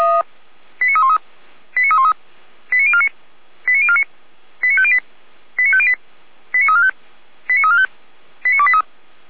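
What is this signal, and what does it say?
Five-tone selective call (ZVEI), heard as quick runs of five tones, tones of about 70 ms. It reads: 89313 89313 89049 89049 89508 89508 89345 89345 89283
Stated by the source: ZVEI five-tone selective call 89337 received on SDR, from fire brigade Feuerwehr Orlamünde.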